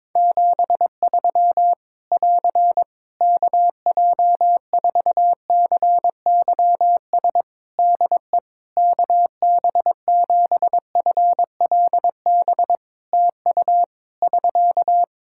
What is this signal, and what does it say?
Morse code, heard as a keyed tone, about 22 wpm, 700 Hz. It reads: 73 <AR> KJ4CYS DE KB7FLB TU <SK>
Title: Practice QSO - Exchange 4 - Ending - 22wpm